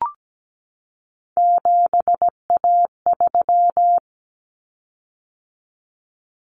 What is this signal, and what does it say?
Morse code, heard as a keyed tone, about 17 wpm, 700 Hz. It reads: 7A3
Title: Common Cut Numbers - 3-Digit 17wpm